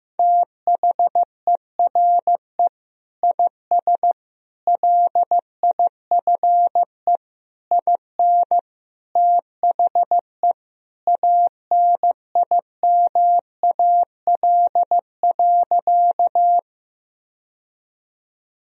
Morse code, heard as a keyed tone, about 15 wpm, 700 Hz. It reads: THERE IS LIFE IN THE ANIMAL.
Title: Sentences from Top 1000 Words - Review Part 1 - 15wpm